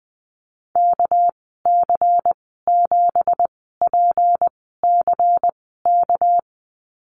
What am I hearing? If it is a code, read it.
Morse code, heard as a keyed tone, about 20 wpm, spaced as for 14 wpm, 700 Hz. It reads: KC7PCK
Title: 231 Lesson 37g - Letter x - Call Signs 20wpm